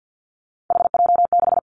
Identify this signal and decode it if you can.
Morse code, heard as a keyed tone, about 50 wpm, 700 Hz. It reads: HJ6
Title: Callsign Prefixes for Top 100 DXCC - 50wpm